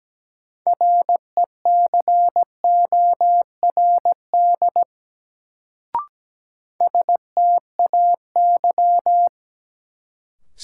Morse code, heard as a keyed tone, about 17 wpm, 700 Hz. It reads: RECORD STAY
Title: Sentences from Top 500 Words - New Word Drill - 17wpm